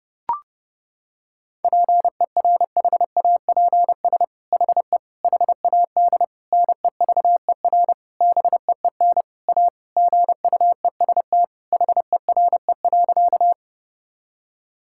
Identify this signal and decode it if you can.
Morse code, heard as a keyed tone, about 30 wpm, 700 Hz. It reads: PERHAPS HE HAD NEVER BEEN A GUEST HERE.
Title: Sentences from Top 1600 Words 30wpm